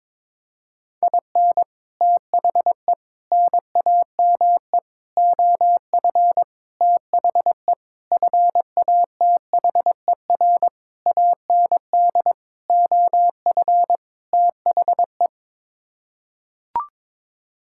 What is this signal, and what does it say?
Morse code, heard as a keyed tone, about 21 wpm, 700 Hz. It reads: IN THE NAME OF THE FATHER AND OF THE
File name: Sets of 9 Words - 22wpm